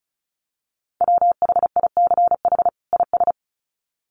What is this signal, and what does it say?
Morse code, heard as a keyed tone, about 34 wpm, 700 Hz. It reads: WHICH IS
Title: Sets of 2 Words 35wpm